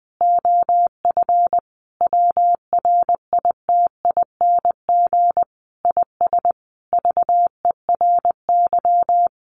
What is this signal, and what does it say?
Morse code, heard as a keyed tone, about 20 wpm, 700 Hz. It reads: OF WRITING IS VERY
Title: Sentences from Top 800 Words 20wpm